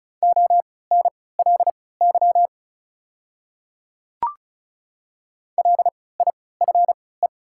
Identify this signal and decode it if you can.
Morse code, heard as a keyed tone, about 36 wpm, spaced as for 20 wpm, 700 Hz. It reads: ONLY LIFE
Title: 3x Character Spacing - Top 300 Words - 35wpm